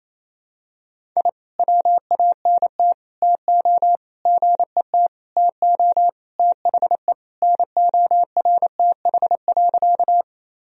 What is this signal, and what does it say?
Morse code, heard as a keyed tone, about 28 wpm, 700 Hz. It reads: I WANT TO GET TO THE NORTH.
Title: Sentences from Top 1000 Words - Review Part 1 - 28wpm